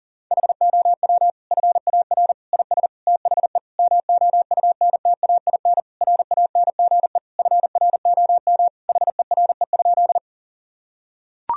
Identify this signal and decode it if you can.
Morse code, heard as a keyed tone, about 40 wpm, 700 Hz. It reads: HOW FAR IS THE MOUNTAIN RANGE FROM HERE?
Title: Sentences from Top 1000 Words - Review Part 4 - 40wpm